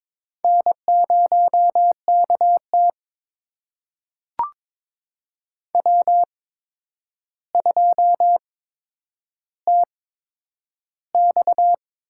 Morse code, heard as a keyed tone, about 22 wpm, spaced as for 9 wpm, 700 Hz. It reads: N0KT W2TX
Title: USA Call Signs - 4 Characters - 8x Character Spacing - 22wpm